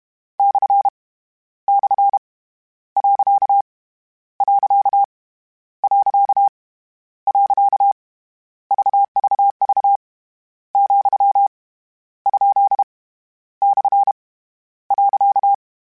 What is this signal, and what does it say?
Morse code, heard as a keyed tone, about 32 wpm, 800 Hz. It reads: / / . . VVV , ? / .
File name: Misc / Misc-mp3s / Punct @30